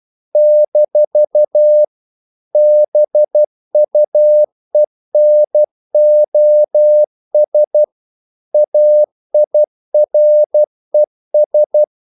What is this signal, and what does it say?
Morse code, heard as a keyed tone, about 12 wpm, 600 Hz. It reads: - BUENOS AIRES
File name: CW / Practice / NationalCapitals12